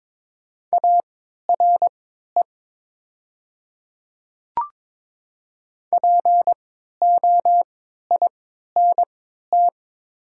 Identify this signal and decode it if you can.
Morse code, heard as a keyed tone, about 22 wpm, spaced as for 13 wpm, 700 Hz. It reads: ARE POINT
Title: 3x Character Spacing - Top 200 Words - 22wpm